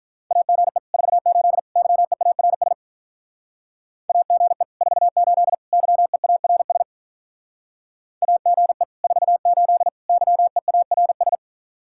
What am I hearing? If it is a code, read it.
Morse code, heard as a keyed tone, about 44 wpm, 700 Hz. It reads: AGE 48 YEARS AGE 48 YEARS AGE 48 YEARS